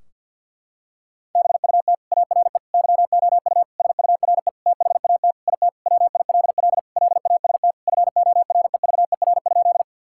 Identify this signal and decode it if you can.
Morse code, heard as a keyed tone, about 46 wpm, 700 Hz. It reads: BUT ARE YOU SURE THAT IT WILL LAST FOREVER?
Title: Sentences from Top 1600 Words 50wpm